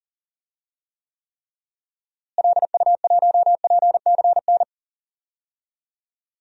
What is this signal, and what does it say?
Morse code, heard as a keyed tone, about 39 wpm, 700 Hz. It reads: LU1PCN
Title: DXCC-100 Call Signs - 6 Characters - Repeated 3x - 40wpm